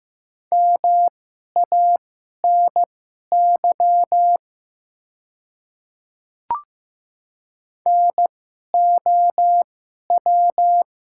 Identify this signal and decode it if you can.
Morse code, heard as a keyed tone, about 15 wpm, spaced as for 11 wpm, 700 Hz. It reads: MANY NOW
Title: 2x Character Spacing - Top 100 Words - 15wpm